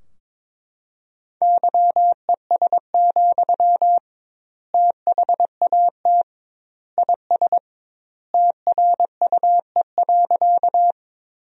Morse code, heard as a keyed tone, about 22 wpm, 700 Hz. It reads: YES, THAT IS TRUE.
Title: 2x Word-Spacing - Sentences from Top 500 Words - 22wpm